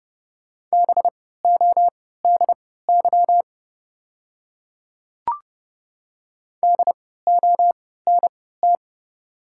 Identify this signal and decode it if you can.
Morse code, heard as a keyed tone, about 31 wpm, spaced as for 17 wpm, 690 Hz. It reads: BODY DONT
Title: 3x Character Spacing - Top 500 Words - 30wpm